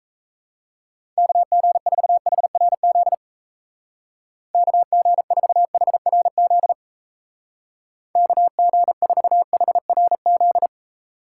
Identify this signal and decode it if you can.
Morse code, heard as a keyed tone, about 35 wpm, 700 Hz. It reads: KG4HRZ KG4HRZ KG4HRZ
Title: Speed Racing - USA Call Signs 28wpm